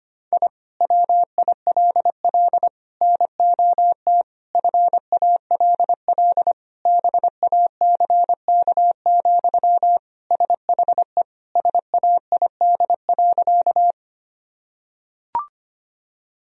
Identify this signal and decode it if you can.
Morse code, heard as a keyed tone, about 25 wpm, 700 Hz. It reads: I WILL NOT FALL BACK, SHE SAID.
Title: Sentences from Top 1000 Words - Review Part 2 - 25wpm